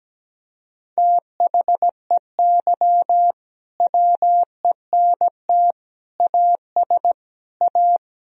Morse code, heard as a keyed tone, about 17 wpm, 700 Hz. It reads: THEY WENT AS A
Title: Sentences from Top 1000 Words - Review Part 1 - 17wpm